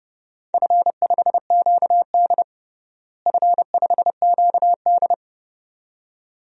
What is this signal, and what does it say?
Morse code, heard as a keyed tone, about 30 wpm, 700 Hz. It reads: F5QD F5QD
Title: DXCC-100 Call Signs - 4 Characters - Repeated 2x - 30wpm